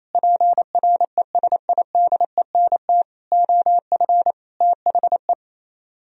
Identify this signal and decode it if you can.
Morse code, heard as a keyed tone, about 35 wpm, 700 Hz. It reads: PRESIDENT OF THE
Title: Speed-Racing - Sets of 4 Words 28wpm